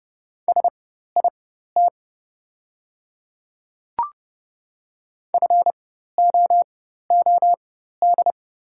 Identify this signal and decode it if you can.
Morse code, heard as a keyed tone, about 30 wpm, spaced as for 14 wpm, 700 Hz. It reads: SIT FOOD
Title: 4x Character Spacing - Top 500 Words - 30wpm